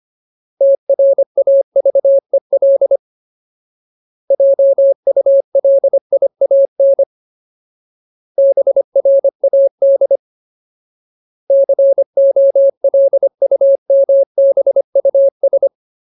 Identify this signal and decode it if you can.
Morse code, heard as a keyed tone, about 25 wpm, 550 Hz. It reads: TRAVEL JULIAN BRAD COLUMBUS
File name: Ultimate Word List / 25WPM Ultimate